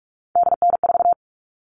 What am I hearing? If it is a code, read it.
Morse code, heard as a keyed tone, about 44 wpm, 700 Hz. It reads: DN4
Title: All Cut Numbers - 3-Digit 45wpm